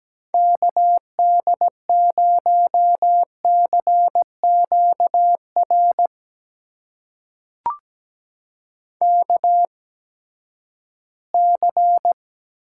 Morse code, heard as a keyed tone, about 17 wpm, spaced as for 12 wpm, 700 Hz. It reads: KD0CQR KC
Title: USA Call Signs - 6 Characters - 8x Character Spacing - 17wpm